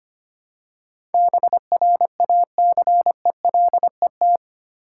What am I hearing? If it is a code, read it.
Morse code, heard as a keyed tone, about 25 wpm, 700 Hz. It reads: BRACELET
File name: practice / 25wpm / Easy Words with Hints - 25wpm